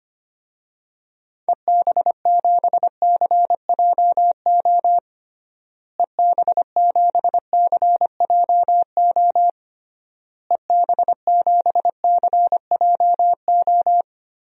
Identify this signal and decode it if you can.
Morse code, heard as a keyed tone, about 25 wpm, 690 Hz. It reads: EB7CJO EB7CJO EB7CJO
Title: DXCC-100 Call Signs - 6 Characters - Repeated 3x - 25wpm